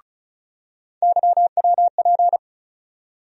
Morse code, heard as a keyed tone, about 36 wpm, 700 Hz. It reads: YWP
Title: USA Call Sign Postfixes - 3 Letters 35wpm